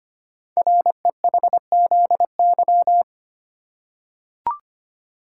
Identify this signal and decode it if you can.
Morse code, heard as a keyed tone, about 25 wpm, 700 Hz. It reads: REHZY
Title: Code Groups - 5 Letters - 25wpm